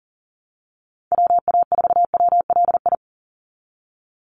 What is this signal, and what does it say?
Morse code, heard as a keyed tone, about 40 wpm, 700 Hz. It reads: WA4WLI
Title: Call Signs - 40wpm